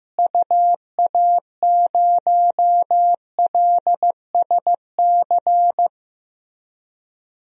Morse code, heard as a keyed tone, about 15 wpm, 700 Hz. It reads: UA0LSC
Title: DXCC-100 Call Signs - 6 Characters - Repeated 3x - 15wpm